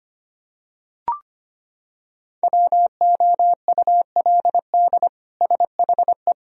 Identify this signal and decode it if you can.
Morse code, heard as a keyed tone, about 25 wpm, 700 Hz. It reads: WOULD SHE
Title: Sentences from Top 1000 Words - Review Part 2 - 25wpm